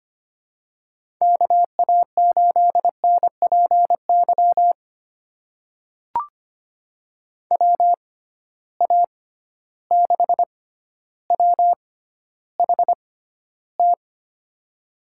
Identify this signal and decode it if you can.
Morse code, heard as a keyed tone, about 25 wpm, spaced as for 13 wpm, 700 Hz. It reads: KA8NPY WA6WHT
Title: USA Call Signs - 6 Characters - 6x Character Spacing - 25wpm